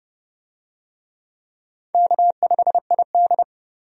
Morse code, heard as a keyed tone, about 32 wpm, 700 Hz. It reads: K5ID
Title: USA Call Signs - 4 Characters - 30wpm